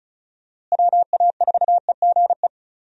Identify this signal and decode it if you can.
Morse code, heard as a keyed tone, about 34 wpm, 700 Hz. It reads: WA4EGE
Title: USA Call Signs - 6 Characters - Repeated 3x - 35wpm